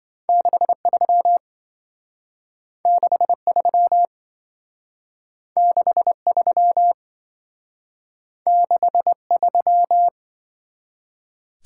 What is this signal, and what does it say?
Morse code, heard as a keyed tone, about 26 wpm, 700 Hz. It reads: 63 63 63 63